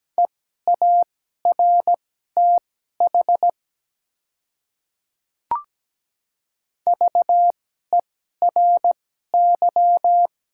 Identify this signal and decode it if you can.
Morse code, heard as a keyed tone, about 17 wpm, spaced as for 12 wpm, 700 Hz. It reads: EARTH VERY
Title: 2x Character Spacing - Top 200 Words - 17wpm